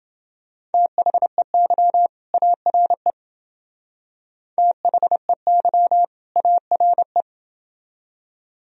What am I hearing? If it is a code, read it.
Morse code, heard as a keyed tone, about 28 wpm, 700 Hz. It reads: THEY ARE THEY ARE